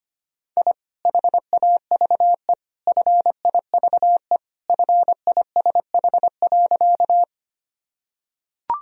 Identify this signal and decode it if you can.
Morse code, heard as a keyed tone, about 25 wpm, 700 Hz. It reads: I HAVE FIVE FISH.
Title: Sentences from Top 1000 Words - Review Part 2 - 25wpm